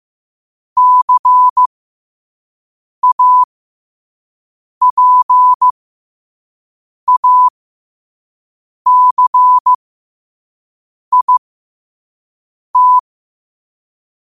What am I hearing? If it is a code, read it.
Morse code, heard as a keyed tone, about 15 wpm, spaced as for 5 wpm, 1000 Hz. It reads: CAPACIT
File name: qotdfiles / 5wpm